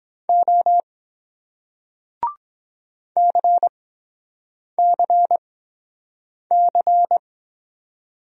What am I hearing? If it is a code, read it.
Morse code, heard as a keyed tone, about 24 wpm, 700 Hz. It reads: O C C C